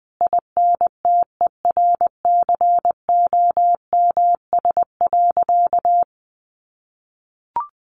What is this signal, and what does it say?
Morse code, heard as a keyed tone, about 19 wpm, 700 Hz. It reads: INTERCOMS.
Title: A Sound Of Thunder 20wpm